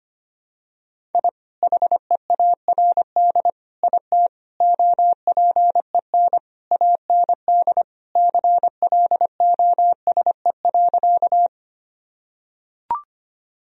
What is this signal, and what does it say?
Morse code, heard as a keyed tone, about 25 wpm, 700 Hz. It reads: I HEARD IT OPEN AND CLOSE.
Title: Sentences from Top 1000 Words - Review Part 1 - 25wpm